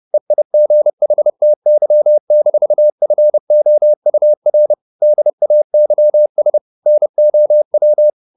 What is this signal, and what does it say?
Morse code, heard as a keyed tone, about 30 wpm, 600 Hz. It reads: EIGHTY-FOUR DAYS NOW
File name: the old man and the sea / the old man and the sea 1 30wpm